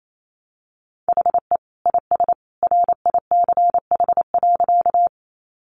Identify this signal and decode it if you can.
Morse code, heard as a keyed tone, about 28 wpm, 700 Hz. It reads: HE IS RICH.
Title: Sentences from Top 1100 Words 28wpm